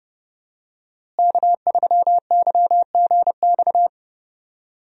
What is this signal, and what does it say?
Morse code, heard as a keyed tone, about 30 wpm, 700 Hz. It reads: K3YGX
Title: USA Call Signs - 5 Characters - Repeated 2x - 30wpm